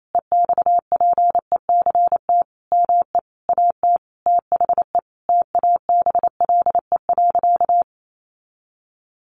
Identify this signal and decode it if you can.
Morse code, heard as a keyed tone, about 28 wpm, 700 Hz. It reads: EXPECT ME AT THE TABLE.
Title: Sentences from Top 1100 Words 28wpm